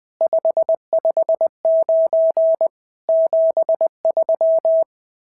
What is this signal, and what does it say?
Morse code, heard as a keyed tone, about 20 wpm, 650 Hz. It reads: 559 73
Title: SOTA QSO Practice 20wpm